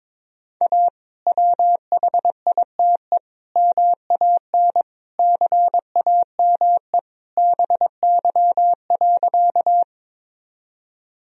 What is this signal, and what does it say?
Morse code, heard as a keyed tone, about 22 wpm, 700 Hz. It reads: A WHITE MAN CAME BY.
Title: Sentences from Top 1000 Words - Review Part 1 - 22wpm